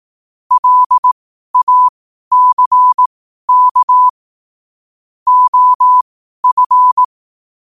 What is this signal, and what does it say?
Morse code, heard as a keyed tone, about 18 wpm, spaced as for 12 wpm, 1000 Hz. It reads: LACK OF
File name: qotdfiles / 12wpm